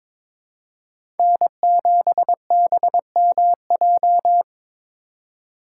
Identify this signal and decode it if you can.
Morse code, heard as a keyed tone, about 22 wpm, 700 Hz. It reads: N7BMJ
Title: USA Call Signs - 5 Characters - 4x Character Spacing - 22wpm